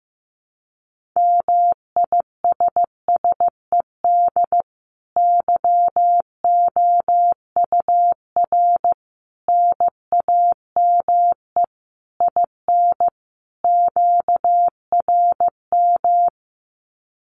Morse code, heard as a keyed tone, about 15 wpm, 700 Hz. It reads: MISSED YOUR NAME IN QRM
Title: Common QSO Phrases 15wpm